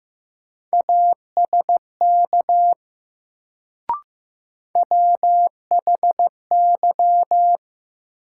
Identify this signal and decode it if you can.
Morse code, heard as a keyed tone, about 15 wpm, 700 Hz. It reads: ASK WHY
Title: Sentences from Top 200 Words - New Word Drill - 15wpm